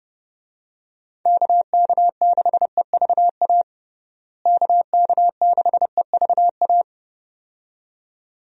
Morse code, heard as a keyed tone, about 29 wpm, 700 Hz. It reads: KK6EVA KK6EVA
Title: USA Call Signs - 6 Characters - Repeated 2x - 30wpm